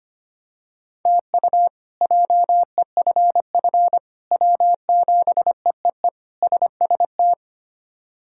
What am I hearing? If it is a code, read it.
Morse code, heard as a keyed tone, about 25 wpm, 700 Hz. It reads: TU JEFF W7EEE SST
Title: K1USN SST QSO Practice - 25wpm